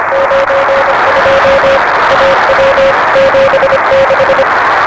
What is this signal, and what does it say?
Morse code, heard as a keyed tone, about 25 wpm, 560 Hz, with noise. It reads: 92 AW 76
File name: RS7 Beacon 16 Feb 1985